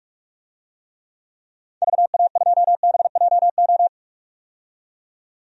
Morse code, heard as a keyed tone, about 46 wpm, 700 Hz. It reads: VA2BJO
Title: DXCC-100 Call Signs - 6 Characters - 45wpm